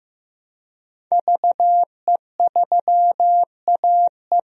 Morse code, heard as a keyed tone, about 15 wpm, 700 Hz. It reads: VE3AE
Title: DXCC-100 Call Signs - 5 Characters - 15wpm